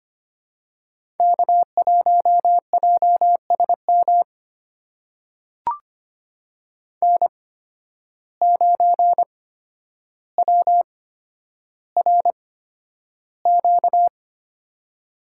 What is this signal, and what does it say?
Morse code, heard as a keyed tone, about 25 wpm, spaced as for 11 wpm, 700 Hz. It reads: K1JSM N9WRQ
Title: USA Call Signs - 5 Characters - 8x Character Spacing - 25wpm